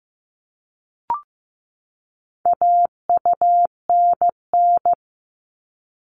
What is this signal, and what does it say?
Morse code, heard as a keyed tone, about 15 wpm, 700 Hz. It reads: AUNN
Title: All Cut Numbers - 4-Digit 15wpm